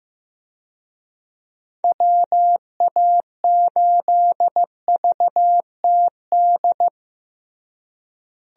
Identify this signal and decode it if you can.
Morse code, heard as a keyed tone, about 14 wpm, 700 Hz. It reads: WA8VTD
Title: USA Call Signs - 6 Characters - Repeated 2x - 15wpm